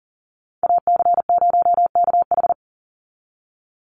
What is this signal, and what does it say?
Morse code, heard as a keyed tone, about 40 wpm, 700 Hz. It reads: AC0KH
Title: USA Call Signs 40wpm